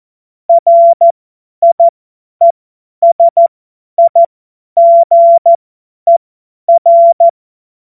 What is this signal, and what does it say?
Morse code, heard as a keyed tone, about 14 wpm, spaced as for 10 wpm, 680 Hz. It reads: RIESIGER
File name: Morsekurs / Witze / Witz 01. März 2023 50 F70